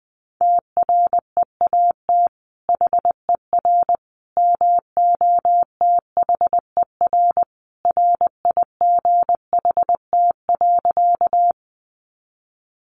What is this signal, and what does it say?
Morse code, heard as a keyed tone, about 20 wpm, 700 Hz. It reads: TREAT HER MOTHER RIGHT.
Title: Sentences from Top 1300 Words 20wpm